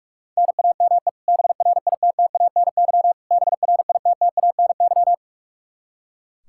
Morse code, heard as a keyed tone, about 46 wpm, 700 Hz. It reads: NAME BRITTANY BRITTANY
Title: Practice QSO - Exchange 1 - Beginning - 45wpm